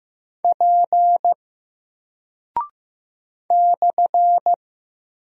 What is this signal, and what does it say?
Morse code, heard as a keyed tone, about 15 wpm, 700 Hz. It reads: P /
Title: Single Character (Letter:Number:Punctuation:Prosign) 15wpm